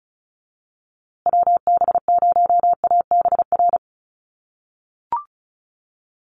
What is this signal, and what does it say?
Morse code, heard as a keyed tone, about 34 wpm, 700 Hz. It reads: WB0ABR
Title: USA Call Signs Repeated 3x 35wpm